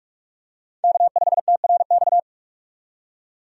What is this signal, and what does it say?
Morse code, heard as a keyed tone, about 46 wpm, 700 Hz. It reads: K5TRX
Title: USA Call Signs - 5 Characters - 45wpm